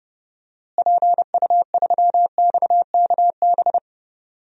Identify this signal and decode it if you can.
Morse code, heard as a keyed tone, about 31 wpm, 700 Hz. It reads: PU3XKB
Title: DXCC-100 Call Signs - 6 Characters - Repeated 2x - 30wpm